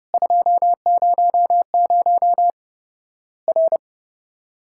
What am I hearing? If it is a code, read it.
Morse code, heard as a keyed tone, about 30 wpm, 700 Hz. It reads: 200 R
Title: Worked All Europe DX Contest - QTC Exchange - 30wpm